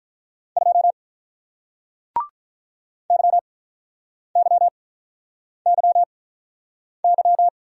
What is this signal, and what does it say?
Morse code, heard as a keyed tone, about 48 wpm, spaced as for 9 wpm, 700 Hz. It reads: 2 YYYY